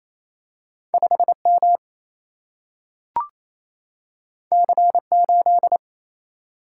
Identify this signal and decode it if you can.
Morse code, heard as a keyed tone, about 28 wpm, 700 Hz. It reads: 5 M C 8
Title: Single-Letter Single-Number 28wpm